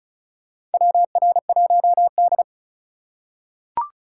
Morse code, heard as a keyed tone, about 34 wpm, 700 Hz. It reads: WR1D
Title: USA Call Signs - 4 Characters - 3x Character Spacing - 35wpm